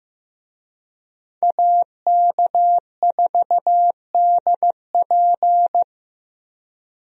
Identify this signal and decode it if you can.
Morse code, heard as a keyed tone, about 15 wpm, 700 Hz. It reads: AK4DP
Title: USA Call Signs - 5 Characters - Repeated 3x - 15wpm